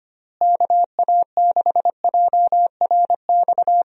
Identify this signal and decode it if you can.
Morse code, heard as a keyed tone, about 25 wpm, 700 Hz. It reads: KA6JRX
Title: USA Call Signs - 6 Characters - Repeated 3x - 25wpm